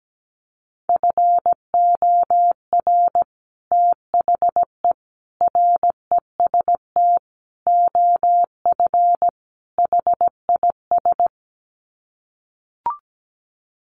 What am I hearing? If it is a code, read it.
Morse code, heard as a keyed tone, about 17 wpm, 700 Hz. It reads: FOR THE REST OF HIS
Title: Sets of 5 Words 17wpm